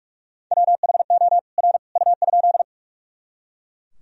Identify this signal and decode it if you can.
Morse code, heard as a keyed tone, about 41 wpm, 700 Hz. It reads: WHO R U?
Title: Sentences from Top 100 Words and Abbreviations 45wpm